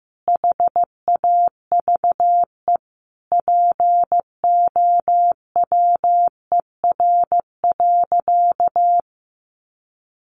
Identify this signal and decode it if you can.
Morse code, heard as a keyed tone, about 14 wpm, 700 Hz. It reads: HAVE POWER.
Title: Sentences from Top 1100 Words 15wpm